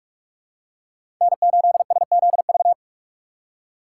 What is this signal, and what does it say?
Morse code, heard as a keyed tone, about 46 wpm, 700 Hz. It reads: N8SZV
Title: USA Call Signs - 5 Characters - 3x Character Spacing - 45wpm